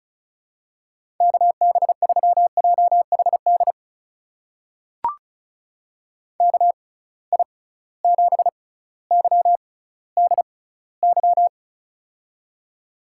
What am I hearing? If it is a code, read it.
Morse code, heard as a keyed tone, about 36 wpm, spaced as for 18 wpm, 700 Hz. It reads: KB3JHD KI7YDY
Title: USA Call Signs - 6 Characters - 6x Character Spacing - 35wpm